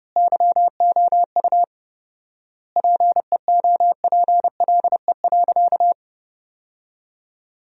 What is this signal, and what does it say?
Morse code, heard as a keyed tone, about 30 wpm, 700 Hz. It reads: YOU PEOPLE.
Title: 4x Word-Spacing - Sentences from Top 300 Words - 30wpm